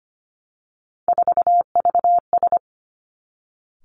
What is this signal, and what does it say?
Morse code, heard as a keyed tone, about 25 wpm, 700 Hz. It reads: 4VS
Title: Three Challenging Letter-Number Combinations 25wpm